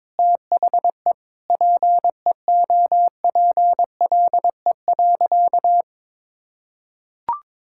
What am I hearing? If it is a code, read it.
Morse code, heard as a keyed tone, about 22 wpm, 700 Hz. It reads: THE PEOPLE.
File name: Sentences with 9-Word Phrases - 22wpm